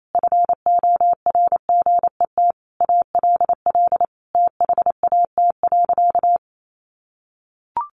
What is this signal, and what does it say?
Morse code, heard as a keyed tone, about 28 wpm, 700 Hz. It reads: FORGET ALL THAT.
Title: Sentences from Top 1200 Words 28wpm